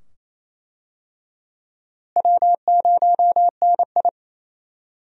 Morse code, heard as a keyed tone, about 28 wpm, 700 Hz. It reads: W0NI
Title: USA Call Signs - 4 Characters - Repeated 2x - 28wpm